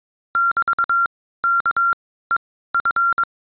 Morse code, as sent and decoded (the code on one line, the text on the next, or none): -...- -.- . ..-.
<BT>KEF